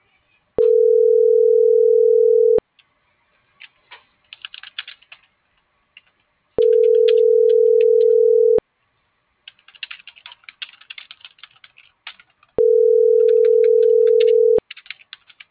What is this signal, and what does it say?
Ringback tone (North America), 440 + 480 Hz.